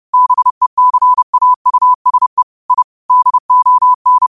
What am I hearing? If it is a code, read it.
Morse code, heard as a keyed tone, about 30 wpm, 1000 Hz. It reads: BECAUSE I DON